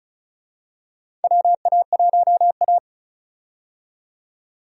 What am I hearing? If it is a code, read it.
Morse code, heard as a keyed tone, about 34 wpm, 700 Hz. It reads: WA1A
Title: USA Call Signs - 4 Characters - 35wpm